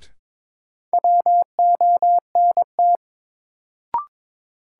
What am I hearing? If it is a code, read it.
Morse code, heard as a keyed tone, about 21 wpm, 700 Hz. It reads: WONT
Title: Sentences from Top 1000 Words - Review Part 6 - New Word Drill - 22wpm